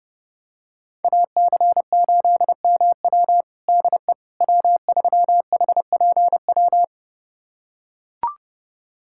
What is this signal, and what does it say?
Morse code, heard as a keyed tone, about 31 wpm, 700 Hz. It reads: AC8MW DE W3HPW
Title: USA Callsign DE Callsign - 5 Characters - 30wpm